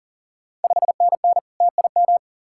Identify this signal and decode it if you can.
Morse code, heard as a keyed tone, about 40 wpm, 690 Hz. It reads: 5NN TIM
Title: Ontario QSO Party Exchange 40wpm